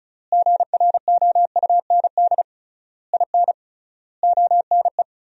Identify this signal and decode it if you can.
Morse code, heard as a keyed tone, about 34 wpm, 700 Hz. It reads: GROUND IN ONE
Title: 3x Word-Spacing - Sentences from Top 500 Words - 35wpm